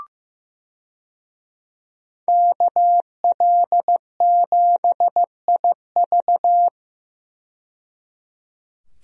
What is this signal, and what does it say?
Morse code, heard as a keyed tone, about 15 wpm, 700 Hz. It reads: KL7IV
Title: DXCC-100 Call Signs - 5 Characters - 15wpm